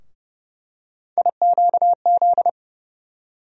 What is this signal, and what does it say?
Morse code, heard as a keyed tone, about 30 wpm, 700 Hz. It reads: IQZ
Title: USA Call Sign Postfixes - 3 Letters 30wpm